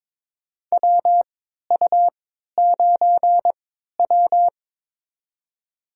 Morse code, heard as a keyed tone, about 22 wpm, spaced as for 13 wpm, 700 Hz. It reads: WU9W
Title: USA Call Signs - 4 Characters - 3x Character Spacing - 22wpm